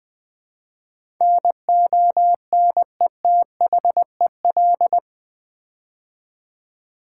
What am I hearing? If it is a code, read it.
Morse code, heard as a keyed tone, about 20 wpm, 700 Hz. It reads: NONETHEL
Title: Build A Word - 4 Letters At A Time - 11-Letter Words - 20wpm